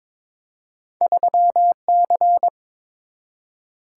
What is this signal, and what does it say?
Morse code, heard as a keyed tone, about 22 wpm, 700 Hz. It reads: 3C
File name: practice / 22wpm / Two Letter-Number Combinations - 22wpm